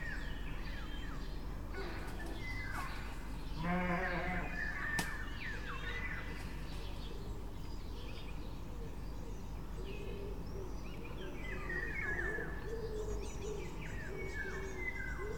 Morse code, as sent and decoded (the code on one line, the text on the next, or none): none